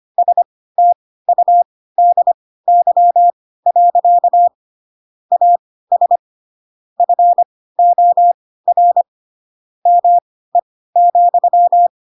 Morse code, heard as a keyed tone, about 25 wpm, spaced as for 16 wpm, 700 Hz. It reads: STUDY. AS FOR ME,
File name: Pinocchio / Pinocchio ch4 10 25 ik1hge